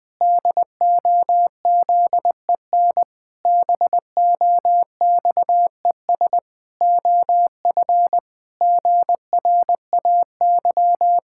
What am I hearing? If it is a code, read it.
Morse code, heard as a keyed tone, about 20 wpm, 700 Hz. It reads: DOZEN BOXES OF GRAY
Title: Pangram Challenge v2 - 20wpm